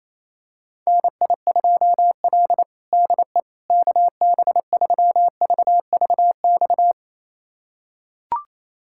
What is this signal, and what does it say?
Morse code, heard as a keyed tone, about 28 wpm, 700 Hz. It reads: NI2L DE KB3VVX
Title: USA Callsign DE Callsign 28wpm